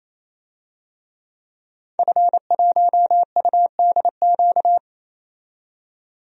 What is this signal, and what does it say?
Morse code, heard as a keyed tone, about 28 wpm, 700 Hz. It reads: F1UDQ
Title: DXCC-100 Call Signs - 5 Characters - 28wpm